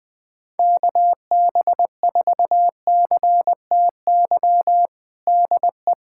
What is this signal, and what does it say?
Morse code, heard as a keyed tone, about 21 wpm, 700 Hz. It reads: KB4CTY DE
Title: USA Callsign DE Callsign 20wpm